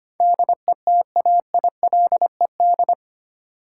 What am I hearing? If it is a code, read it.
Morse code, heard as a keyed tone, about 25 wpm, 700 Hz. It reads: DETAILED
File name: 8 Letter Words - First 3 Letters Given - 25wpm